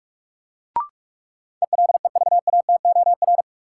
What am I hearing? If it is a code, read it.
Morse code, heard as a keyed tone, about 46 wpm, 700 Hz. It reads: ELEVATOR
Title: Sentences from Top 1600 Words - New Word Drill - 45wpm